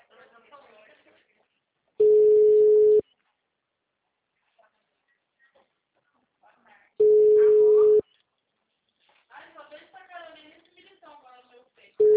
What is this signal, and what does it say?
Ringback tone (Europe, 425 Hz), with speech.